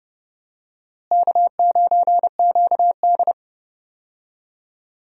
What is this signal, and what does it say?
Morse code, heard as a keyed tone, about 31 wpm, 700 Hz. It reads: K9QD
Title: USA Call Signs - 4 Characters - Repeated 2x - 30wpm